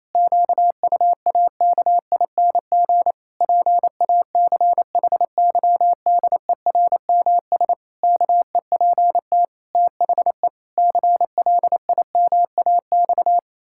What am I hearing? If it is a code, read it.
Morse code, heard as a keyed tone, about 28 wpm, 700 Hz. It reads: QUAKING PACHYDERMS KEPT THE CLIMAX
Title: Pangram Challenge v2 - 28wpm